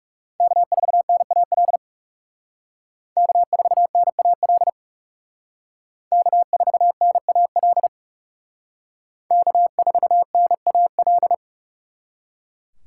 Morse code, heard as a keyed tone, about 36 wpm, 700 Hz. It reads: K4NAL K4NAL K4NAL K4NAL
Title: Speed Racing - USA 1x3 Call Signs 30wpm